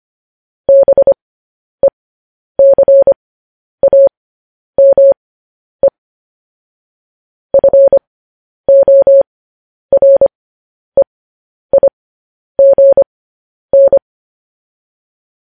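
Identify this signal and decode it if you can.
Morse code, heard as a keyed tone, about 25 wpm, spaced as for 10 wpm, 550 Hz. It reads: BECAME FOREIGN
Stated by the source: Morse code news headlines at 10 wpm.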